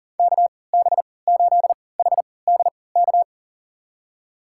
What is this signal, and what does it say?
Morse code, heard as a keyed tone, about 40 wpm, spaced as for 23 wpm, 700 Hz. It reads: KB8HDK
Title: USA Call Signs - 6 Characters - 3x Character Spacing - 40wpm